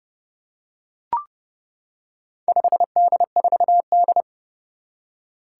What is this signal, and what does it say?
Morse code, heard as a keyed tone, about 31 wpm, 700 Hz. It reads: 5D4D
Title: Four Challenging Letter-Number Combinations 30wpm